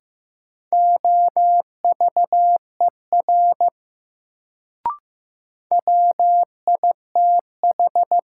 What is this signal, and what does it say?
Morse code, heard as a keyed tone, about 15 wpm, 700 Hz. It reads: OVER WITH
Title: Sentences from Top 100 Words - New Word Drill - 15wpm